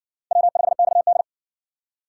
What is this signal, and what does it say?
Morse code, heard as a keyed tone, about 58 wpm, 700 Hz. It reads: W5XD